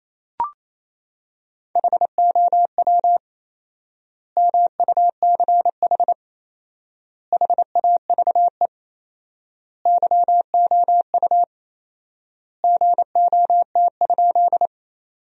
Morse code, heard as a keyed tone, about 28 wpm, 700 Hz. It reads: HOW MUCH HAVE YOU GOT?
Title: 4x Word-Spacing - Sentences from Top 300 Words - 28wpm